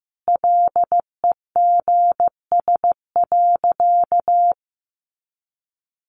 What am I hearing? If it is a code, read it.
Morse code, heard as a keyed tone, about 14 wpm, 700 Hz. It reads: LEGS.
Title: A Sound Of Thunder 15wpm